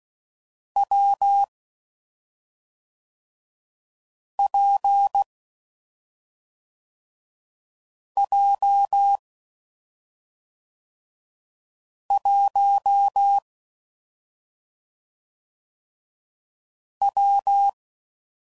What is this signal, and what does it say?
Morse code, heard as a keyed tone, about 16 wpm, 800 Hz. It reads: W P J 1 W
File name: Combinations / Difficult Combos / WPJ WPJ1